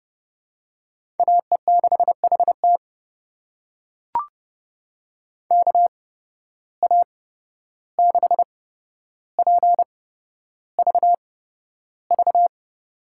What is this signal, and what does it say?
Morse code, heard as a keyed tone, about 30 wpm, spaced as for 12 wpm, 700 Hz. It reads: AE6HT KA6PVV
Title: USA Call Signs - 8x Character Spacing - 30wpm